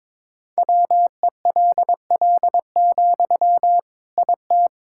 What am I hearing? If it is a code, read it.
Morse code, heard as a keyed tone, about 22 wpm, 700 Hz. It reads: WELL, IT
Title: Sentences from Top 1000 Words - Review Part 6 - 22wpm